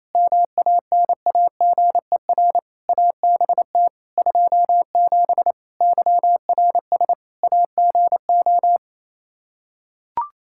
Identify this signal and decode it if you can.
Morse code, heard as a keyed tone, about 27 wpm, 700 Hz. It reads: MANAGER ABT 27 YRS AGO
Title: Practice QSO - Exchange 3 - More Info - 28wpm